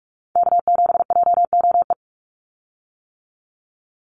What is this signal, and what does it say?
Morse code, heard as a keyed tone, about 46 wpm, 700 Hz. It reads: K7JOE
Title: Call Signs - 45wpm